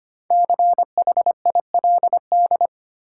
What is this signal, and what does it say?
Morse code, heard as a keyed tone, about 25 wpm, 700 Hz. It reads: CHILD
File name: 5 Letter Words - First 3 Letters Given - 25wpm